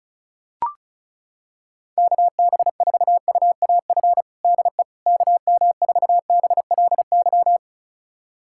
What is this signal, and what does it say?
Morse code, heard as a keyed tone, about 36 wpm, 700 Hz. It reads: KB4UAF DE KM4BLY
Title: USA Callsign DE Callsign 35wpm